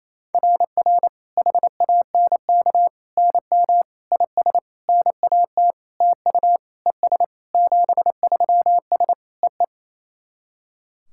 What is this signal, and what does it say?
Morse code, heard as a keyed tone, about 28 wpm, 700 Hz. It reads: RR HANK NM IS NAT TU ES 73S EE
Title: SOTA QSO Practice 28wpm